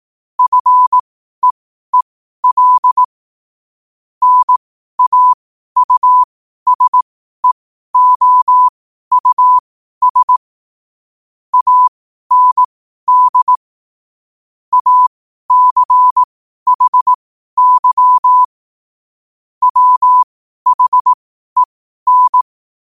Morse code, heard as a keyed tone, about 18 wpm, spaced as for 12 wpm, 1000 Hz. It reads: FEEL NAUSEOUS AND ACHY WHEN
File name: qotdfiles / 12wpm